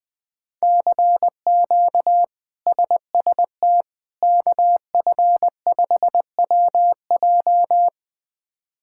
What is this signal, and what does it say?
Morse code, heard as a keyed tone, about 20 wpm, 700 Hz. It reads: CQ SST KF5WJ
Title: K1USN SST QSO Practice - 20wpm